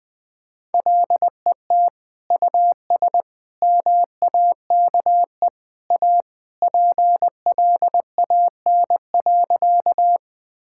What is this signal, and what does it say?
Morse code, heard as a keyed tone, about 20 wpm, 700 Hz. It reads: LET US MAKE A PLAN.
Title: Sentences with 2-Word Phrases - 20wpm